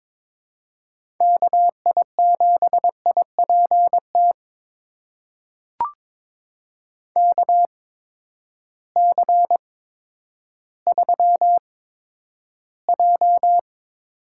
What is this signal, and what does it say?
Morse code, heard as a keyed tone, about 22 wpm, spaced as for 11 wpm, 700 Hz. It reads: KI7IPT KC3J
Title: USA Call Signs - 6 Characters - 8x Character Spacing - 22wpm